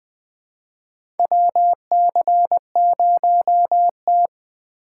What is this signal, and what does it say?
Morse code, heard as a keyed tone, about 20 wpm, 700 Hz. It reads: WC0T
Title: USA Call Signs - 4 Characters - 2x Character Spacing - 20wpm